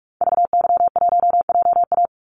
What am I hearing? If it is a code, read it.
Morse code, heard as a keyed tone, about 44 wpm, 700 Hz. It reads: VY1JA
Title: Call Signs Repeated 3x - 45wpm